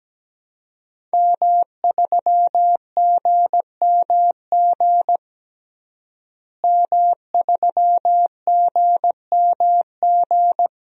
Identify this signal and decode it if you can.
Morse code, heard as a keyed tone, about 17 wpm, 700 Hz. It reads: M3GMG M3GMG
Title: DXCC-100 Call Signs - 5 Characters - Repeated 3x - 17wpm